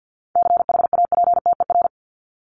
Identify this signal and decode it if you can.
Morse code, heard as a keyed tone, about 50 wpm, 700 Hz. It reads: CHAPTER
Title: Top 1751-2000 Words 50wpm